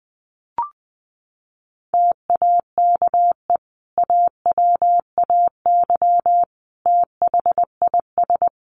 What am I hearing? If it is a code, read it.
Morse code, heard as a keyed tone, about 20 wpm, 700 Hz. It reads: TAKE AWAY THIS
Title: Sentences from Top 1300 Words 20wpm